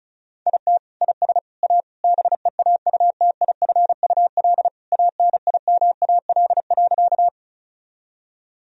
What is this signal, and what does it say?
Morse code, heard as a keyed tone, about 36 wpm, 700 Hz. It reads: IT IS A BEAUTIFUL ANIMAL.
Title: Sentences from Top 1000 Words - Review Part 1 - 35wpm